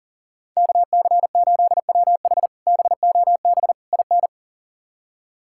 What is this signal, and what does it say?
Morse code, heard as a keyed tone, about 40 wpm, 700 Hz. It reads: KC8WH BOB IN
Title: K1USN SST QSO Practice - 40wpm